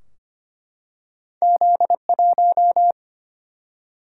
Morse code, heard as a keyed tone, about 25 wpm, 700 Hz. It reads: Z1
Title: Single-Letter Single-Number 25wpm